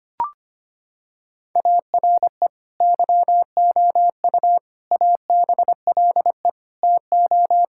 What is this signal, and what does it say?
Morse code, heard as a keyed tone, about 25 wpm, 700 Hz. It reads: ARE YOU ABLE TO
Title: Sentences from Top 1000 Words - Review Part 5 - 25wpm